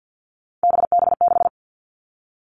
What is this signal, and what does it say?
Morse code, heard as a keyed tone, about 50 wpm, 700 Hz. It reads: BB6
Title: Three Challenging Letter-Number Combinations 50wpm